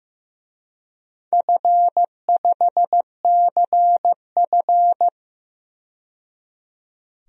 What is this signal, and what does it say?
Morse code, heard as a keyed tone, about 15 wpm, 700 Hz. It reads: F5CF
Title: DXCC-100 Call Signs - 4 Characters - 15wpm